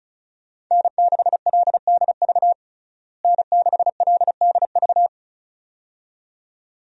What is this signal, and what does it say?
Morse code, heard as a keyed tone, about 36 wpm, 700 Hz. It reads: N6LDV N6LDV
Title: USA Call Signs - 5 Characters - Repeated 2x - 35wpm